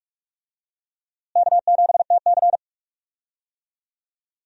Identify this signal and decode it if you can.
Morse code, heard as a keyed tone, about 46 wpm, 700 Hz. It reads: K7TC
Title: USA Call Signs - 4 Characters - Repeated 2x - 45wpm